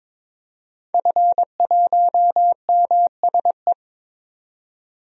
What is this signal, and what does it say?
Morse code, heard as a keyed tone, about 22 wpm, 700 Hz. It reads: F1MSE
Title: DXCC-100 Call Signs - 5 Characters - 22wpm